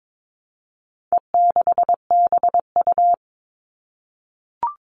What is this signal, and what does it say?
Morse code, heard as a keyed tone, about 22 wpm, 700 Hz. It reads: E6BU
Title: All Cut Numbers - 4-Digit 22wpm